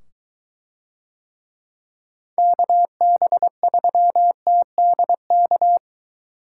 Morse code, heard as a keyed tone, about 23 wpm, 700 Hz. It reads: KB3TDK